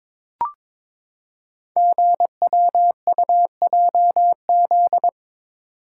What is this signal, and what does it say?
Morse code, heard as a keyed tone, about 22 wpm, 700 Hz. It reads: GWUJZ
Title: Code Groups - 5 Letters - 22wpm